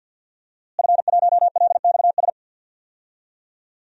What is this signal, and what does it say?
Morse code, heard as a keyed tone, about 50 wpm, 700 Hz. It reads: F1LXS
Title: DXCC-100 Call Signs - 5 Characters - Repeated 3x - 50wpm